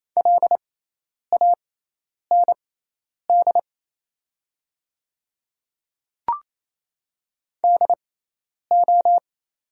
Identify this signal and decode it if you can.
Morse code, heard as a keyed tone, about 28 wpm, spaced as for 10 wpm, 700 Hz. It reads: LAND DO
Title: 6x Character Spacing - Top 200 Words - 28wpm